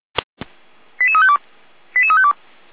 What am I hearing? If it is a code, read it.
Five-tone selective call (ZVEI), heard as quick runs of five tones, tones of about 70 ms. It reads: 88352 88352